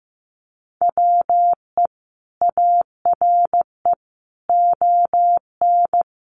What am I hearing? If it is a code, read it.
Morse code, heard as a keyed tone, about 15 wpm, 700 Hz. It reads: WE ARE ON
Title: Sentences from Top 1100 Words 15wpm